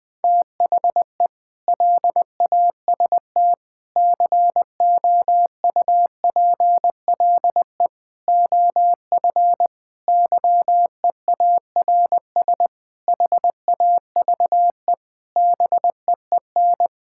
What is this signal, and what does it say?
Morse code, heard as a keyed tone, about 20 wpm, 700 Hz. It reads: THE LAST COUPLE OF YEARS HAVE BEEN
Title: Sentences with 5-Word Phrases - 20wpm